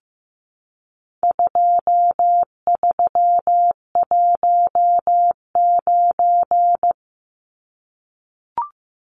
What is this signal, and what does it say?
Morse code, heard as a keyed tone, about 15 wpm, 700 Hz. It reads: 2319
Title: Numbers - 4-Digit 15wpm